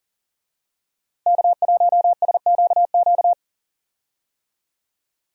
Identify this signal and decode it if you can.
Morse code, heard as a keyed tone, about 40 wpm, 700 Hz. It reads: K1SQQ
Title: USA Call Signs - 5 Characters - Repeated 3x - 40wpm